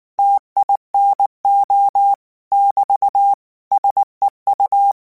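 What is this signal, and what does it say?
Morse code, heard as a keyed tone, about 19 wpm, 800 Hz. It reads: TINO <BT> SEU
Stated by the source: Morse code practice text at 18 wpm, from Portal do CW.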